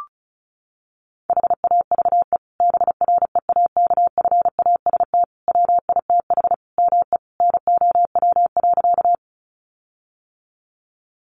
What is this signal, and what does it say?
Morse code, heard as a keyed tone, about 34 wpm, 700 Hz. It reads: HAVE BREAKFAST WITH ME NOW.
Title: Sentences from Top 1500 Words 35wpm